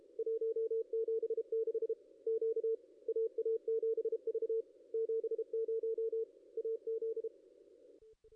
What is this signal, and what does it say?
Morse code, heard as a keyed tone, about 32 wpm, 450 Hz, with radio noise. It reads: 176 Q AA7V 70 AZ
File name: contests / ss-cw-k5tr-op-n5rz -2200